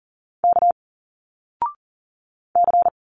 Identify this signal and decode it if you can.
Morse code, heard as a keyed tone, about 39 wpm, 700 Hz. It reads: K C
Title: Single Letters 40wpm